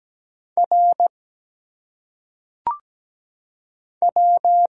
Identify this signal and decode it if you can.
Morse code, heard as a keyed tone, about 17 wpm, 700 Hz. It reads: R W
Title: Single Letters - Phonetic Pronunciation 17wpm